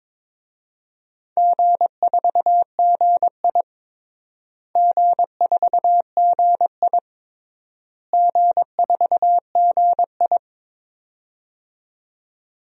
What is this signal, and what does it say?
Morse code, heard as a keyed tone, about 22 wpm, 700 Hz. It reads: G4GI G4GI G4GI
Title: DXCC-100 Call Signs - 4 Characters - Repeated 3x - 22wpm